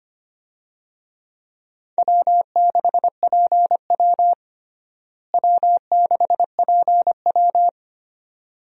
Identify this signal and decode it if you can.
Morse code, heard as a keyed tone, about 25 wpm, 700 Hz. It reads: W6PW W6PW
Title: USA Call Signs - 4 Characters - Repeated 2x - 25wpm